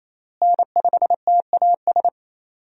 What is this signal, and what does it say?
Morse code, heard as a keyed tone, about 28 wpm, 700 Hz. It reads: N5TAS
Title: USA Call Signs - 5 Characters - Repeated 3x - 28wpm